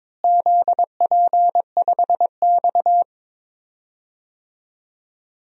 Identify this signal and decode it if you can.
Morse code, heard as a keyed tone, about 22 wpm, 700 Hz. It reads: ZP5X
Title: DXCC-100 Call Signs - 4 Characters - Repeated 2x - 22wpm